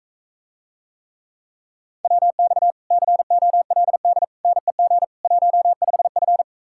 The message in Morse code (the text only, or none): WX COLD NEG 15F